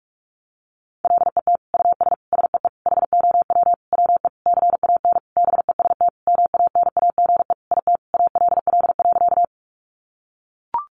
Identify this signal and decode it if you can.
Morse code, heard as a keyed tone, about 44 wpm, 700 Hz. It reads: LET US SEE HOW WE CAN BEST MANAGE IT ALL.
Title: Sentences from Top 1400 Words 45wpm